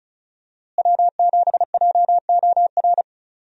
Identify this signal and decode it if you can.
Morse code, heard as a keyed tone, about 35 wpm, 700 Hz. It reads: W7JOR
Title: USA Call Signs - 5 Characters - Repeated 2x - 35wpm